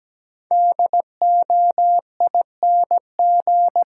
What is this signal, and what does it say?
Morse code, heard as a keyed tone, about 17 wpm, 700 Hz. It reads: DOING
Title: Sets of 4 Words - New Word Drill - 17wpm